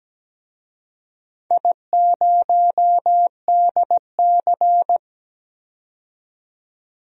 Morse code, heard as a keyed tone, about 17 wpm, 700 Hz. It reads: I0DC
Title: DXCC-100 Call Signs - 4 Characters - 17wpm